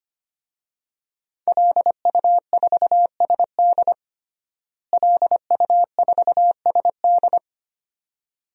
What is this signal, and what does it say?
Morse code, heard as a keyed tone, about 25 wpm, 700 Hz. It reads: LU4SD LU4SD
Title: DXCC-100 Call Signs - 5 Characters - Repeated 2x - 25wpm